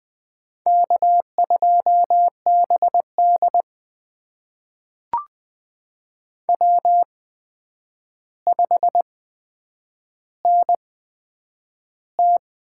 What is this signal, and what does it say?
Morse code, heard as a keyed tone, about 20 wpm, spaced as for 9 wpm, 700 Hz. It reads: K2BD W5NT
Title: USA Call Signs - 4 Characters - 8x Character Spacing - 20wpm